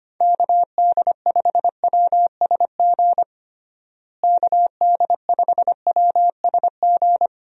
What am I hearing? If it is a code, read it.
Morse code, heard as a keyed tone, about 25 wpm, 700 Hz. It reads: KD5WSG KD5WSG
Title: USA Call Signs - 6 Characters - Repeated 3x - 25wpm